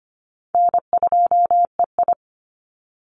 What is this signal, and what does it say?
Morse code, heard as a keyed tone, about 25 wpm, 700 Hz. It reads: N2EI
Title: Easy Call Signs Repeated 3x - 25wpm